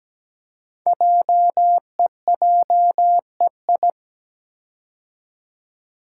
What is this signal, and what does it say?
Morse code, heard as a keyed tone, about 17 wpm, 700 Hz. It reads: JEJEI
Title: Code Groups - 5 Letter:Numbers - 17wpm